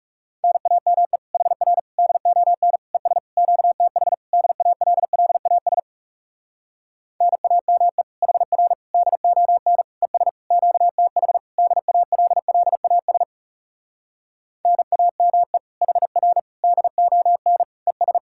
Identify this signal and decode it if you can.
Morse code, heard as a keyed tone, about 40 wpm, 700 Hz. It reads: NAME HR DON ES QTH DALLAS NAME HR DON ES QTH DALLAS NAME HR DON ES